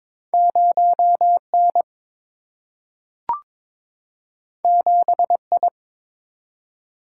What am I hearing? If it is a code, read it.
Morse code, heard as a keyed tone, about 23 wpm, 700 Hz. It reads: 0 N 7 I
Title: Single-Letter Single-Number 22wpm